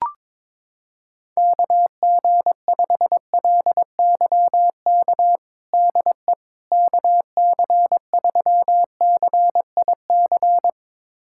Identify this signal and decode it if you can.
Morse code, heard as a keyed tone, about 22 wpm, 700 Hz. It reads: KG5LYK DE KC3CIC
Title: USA Callsign DE Callsign - 6 Characters - 22wpm